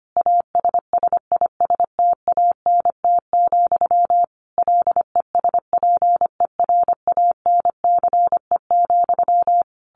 Morse code, heard as a keyed tone, about 25 wpm, 700 Hz. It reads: ASSISTANT, LESPERANCE,
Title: A Sound Of Thunder 25wpm